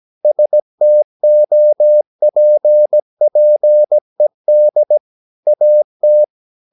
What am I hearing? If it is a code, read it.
Morse code, heard as a keyed tone, about 17 wpm, 600 Hz. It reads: STOPPED AT
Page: Romanen Never af Ken Follet på engelsk.